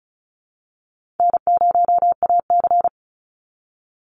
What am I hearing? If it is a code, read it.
Morse code, heard as a keyed tone, about 34 wpm, 700 Hz. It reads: N0AC
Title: Call Signs Repeated 3x - 35wpm